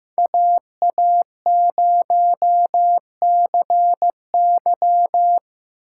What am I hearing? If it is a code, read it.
Morse code, heard as a keyed tone, about 17 wpm, 700 Hz. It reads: AA0CY
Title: Speed Racing - USA Call Signs 15wpm